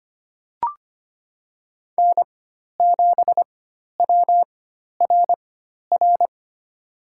Morse code, heard as a keyed tone, about 25 wpm, spaced as for 12 wpm, 700 Hz. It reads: N7WRR
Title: USA Call Signs - 4x Character Spacing - 25wpm